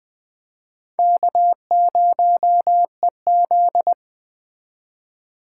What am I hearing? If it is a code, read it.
Morse code, heard as a keyed tone, about 20 wpm, 700 Hz. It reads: K0EZ